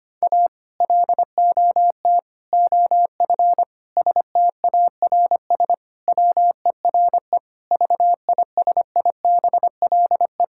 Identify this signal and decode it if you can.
Morse code, heard as a keyed tone, about 25 wpm, 700 Hz. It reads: A LOT OF STARS WERE VISIBLE
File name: practice / 25wpm / Sentences with 2-Word Phrases - 25wpm